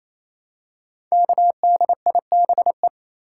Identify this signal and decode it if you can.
Morse code, heard as a keyed tone, about 28 wpm, 700 Hz. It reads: KDIBE
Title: Code Groups - 5 Letters - 28wpm